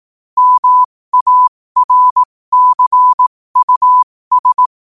Morse code, heard as a keyed tone, about 18 wpm, spaced as for 15 wpm, 1000 Hz. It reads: MARCUS
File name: qotdfiles / 15wpm